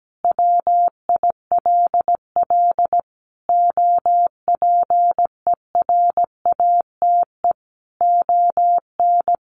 Morse code, heard as a keyed tone, about 17 wpm, 700 Hz. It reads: WILL OPERATE ON
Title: Sentences from Top 1500 Words 17wpm